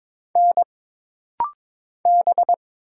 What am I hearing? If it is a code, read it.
Morse code, heard as a keyed tone, about 22 wpm, 700 Hz. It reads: N B
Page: This rapid-fire timing is geared to push your ICR (Instant Character Recognition) skills.